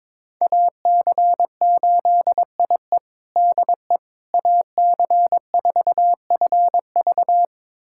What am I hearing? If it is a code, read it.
Morse code, heard as a keyed tone, about 22 wpm, 700 Hz. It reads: AC8IE DE AC4FV
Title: USA Callsign DE Callsign - 5 Characters - 22wpm